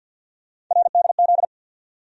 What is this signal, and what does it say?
Morse code, heard as a keyed tone, about 51 wpm, 700 Hz. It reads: RDZ
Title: USA Call Sign Postfixes - 3 Letters 50wpm